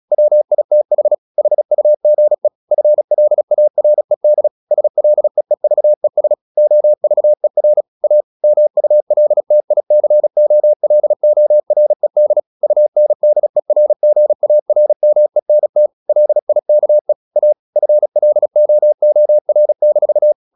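Morse code, heard as a keyed tone, about 36 wpm, 600 Hz. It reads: WITH HUGE FLARED SLEEVES OVER A MULTICOLORED UNDERGARMENT LIKE A FLOOR-
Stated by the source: Morse practice at 36 wpm, text of Ken Follett's novel Never, from CW med Gnister.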